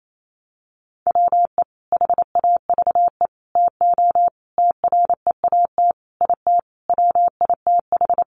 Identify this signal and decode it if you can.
Morse code, heard as a keyed tone, about 28 wpm, 700 Hz. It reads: WE HAVE TO TREAT IT WITH
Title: Sentences from Top 1300 Words 28wpm